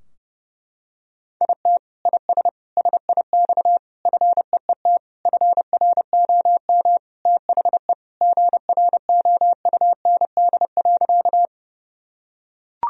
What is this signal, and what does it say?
Morse code, heard as a keyed tone, about 30 wpm, 700 Hz. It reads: IT IS SIX FEET FROM THE GROUND.
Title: Sentences from Top 1000 Words - Review Part 2 - 30wpm